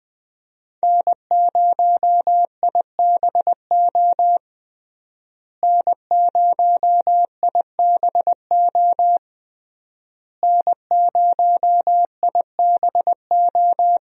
Morse code, heard as a keyed tone, about 20 wpm, 700 Hz. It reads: N0IBO N0IBO N0IBO
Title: USA Call Signs - 5 Characters - Repeated 3x - 20wpm